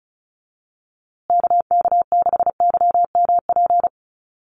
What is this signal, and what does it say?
Morse code, heard as a keyed tone, about 34 wpm, 700 Hz. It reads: KK6YMP
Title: USA Call Signs Repeated 3x 35wpm